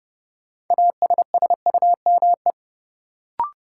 Morse code, heard as a keyed tone, about 30 wpm, 700 Hz. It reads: ASSUME
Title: Sentences from Top 1500 Words - New Word Drill - 30wpm